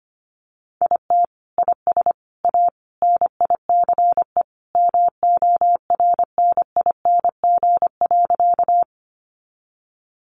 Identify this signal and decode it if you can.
Morse code, heard as a keyed tone, about 25 wpm, 700 Hz. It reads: IT IS A NICE MORNING.
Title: Sentences from Top 1100 Words 25wpm